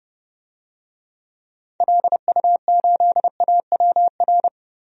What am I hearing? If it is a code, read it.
Morse code, heard as a keyed tone, about 30 wpm, 700 Hz. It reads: LU8AWR
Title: DXCC-100 Call Signs - 6 Characters - 30wpm